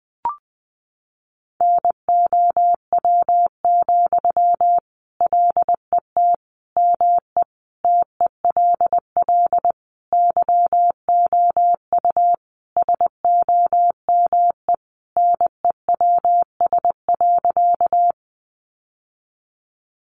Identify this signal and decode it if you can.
Morse code, heard as a keyed tone, about 19 wpm, 700 Hz. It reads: NOW, LET ME TELL YOU SOME NEWS.
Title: Sentences from Top 800 Words 20wpm